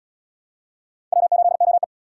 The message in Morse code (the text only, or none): W8OE